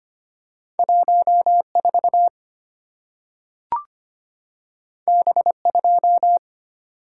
Morse code, heard as a keyed tone, about 25 wpm, 700 Hz. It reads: 14 B2
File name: Two Letter-Number Combinations - 25wpm